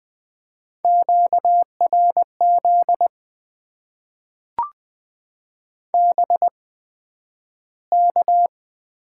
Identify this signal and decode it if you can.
Morse code, heard as a keyed tone, about 20 wpm, spaced as for 11 wpm, 700 Hz. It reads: QRZ BK
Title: Top 90 QSO Elements - 8x Character Spacing - 20wpm